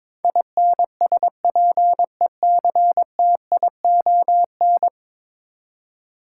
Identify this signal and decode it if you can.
Morse code, heard as a keyed tone, about 22 wpm, 700 Hz. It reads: INSPECTION
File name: practice / 22wpm / 10 Letter Words - First 3 Letters Given - 22wpm